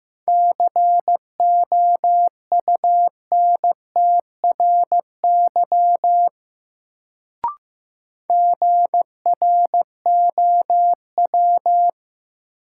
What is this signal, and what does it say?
Morse code, heard as a keyed tone, about 15 wpm, 700 Hz. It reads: COUNTRY GROW
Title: Sentences from Top 300 Words - New Word Drill - 15wpm